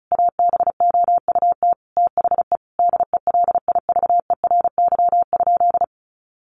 Morse code, heard as a keyed tone, about 34 wpm, 700 Hz. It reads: ABOUT THE DELIVERY?
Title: Sentences from Top 1500 Words 35wpm